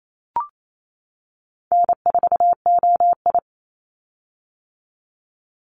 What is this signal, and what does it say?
Morse code, heard as a keyed tone, about 28 wpm, 700 Hz. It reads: N4OI
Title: Call Signs - 28wpm